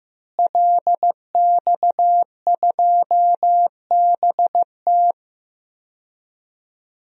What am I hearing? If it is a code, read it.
Morse code, heard as a keyed tone, about 15 wpm, 700 Hz. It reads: LX2BT
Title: DXCC-100 Call Signs - 5 Characters - Repeated 2x - 15wpm